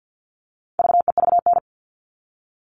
Morse code, heard as a keyed tone, about 50 wpm, 700 Hz. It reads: VEVN
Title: All Cut Numbers - 4-Digit 50wpm